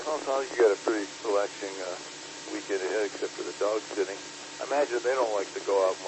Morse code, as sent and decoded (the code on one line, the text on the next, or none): none